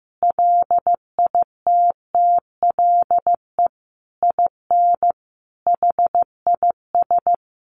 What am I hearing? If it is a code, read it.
Morse code, heard as a keyed tone, about 15 wpm, 700 Hz. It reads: LITTLE IN HIS
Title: Sentences from Top 1400 Words 15wpm